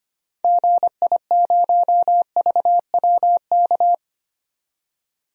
Morse code, heard as a keyed tone, about 25 wpm, 700 Hz. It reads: GI0VWK
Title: DXCC-100 Call Signs - 6 Characters - 25wpm